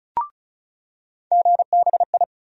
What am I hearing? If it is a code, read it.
Morse code, heard as a keyed tone, about 34 wpm, 700 Hz. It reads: GBI
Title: USA Call Sign Postfixes - 3 Letters 35wpm